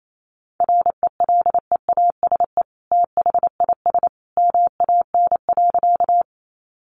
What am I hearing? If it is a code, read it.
Morse code, heard as a keyed tone, about 28 wpm, 700 Hz. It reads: RELEASE THIS MAN.
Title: Sentences from Top 1100 Words 28wpm